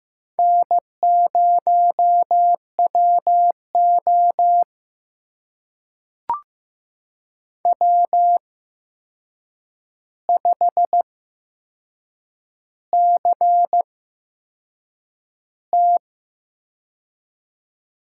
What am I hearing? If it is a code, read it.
Morse code, heard as a keyed tone, about 15 wpm, spaced as for 6 wpm, 700 Hz. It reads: N0WO W5CT
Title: USA Call Signs - 4 Characters - 8x Character Spacing - 15wpm